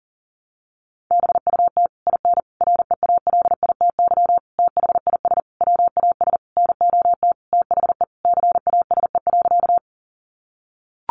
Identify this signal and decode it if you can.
Morse code, heard as a keyed tone, about 40 wpm, 700 Hz. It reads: BUT IN REALITY THIS WAS NOT THE CASE.
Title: Sentences from Top 1200 Words 40wpm